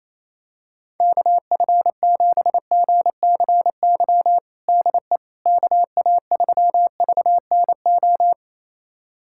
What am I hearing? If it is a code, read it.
Morse code, heard as a keyed tone, about 28 wpm, 700 Hz. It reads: KF7GCY DE KA3VNO
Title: USA Callsign DE Callsign - 6 Characters - 28wpm